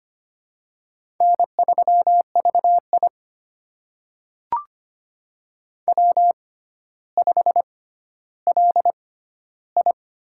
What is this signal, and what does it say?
Morse code, heard as a keyed tone, about 25 wpm, spaced as for 13 wpm, 700 Hz. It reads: N3VI W5LI
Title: USA Call Signs - 4 Characters - 6x Character Spacing - 25wpm